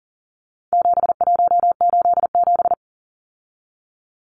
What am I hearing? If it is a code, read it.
Morse code, heard as a keyed tone, about 40 wpm, 700 Hz. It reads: 7187
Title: Numbers - 4-Digit 40wpm